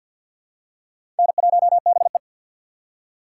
Morse code, heard as a keyed tone, about 52 wpm, 700 Hz. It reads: N1BE
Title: USA Call Signs - 4 Characters - 3x Character Spacing - 50wpm